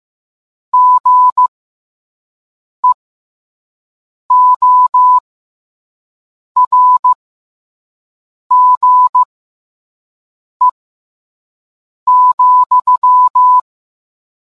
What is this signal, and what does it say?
Morse code, heard as a keyed tone, about 15 wpm, spaced as for 5 wpm, 1000 Hz. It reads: GEORGE,